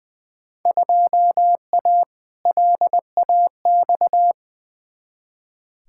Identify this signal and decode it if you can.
Morse code, heard as a keyed tone, about 20 wpm, 700 Hz. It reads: 2A LAX
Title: Field Day Exchanges - 20wpm